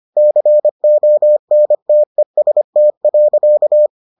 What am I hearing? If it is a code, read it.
Morse code, heard as a keyed tone, about 25 wpm, 600 Hz. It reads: CONTEST.